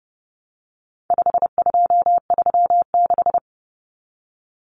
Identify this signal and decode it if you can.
Morse code, heard as a keyed tone, about 30 wpm, 700 Hz. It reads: 5236
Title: Numbers - 4-Digit 30wpm